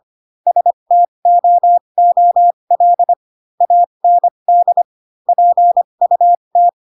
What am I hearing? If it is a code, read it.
Morse code, heard as a keyed tone, about 25 wpm, spaced as for 22 wpm, 700 Hz. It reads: STOOL AND PUT
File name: Pinocchio / Pinocchio ch6 18 25 ik1hge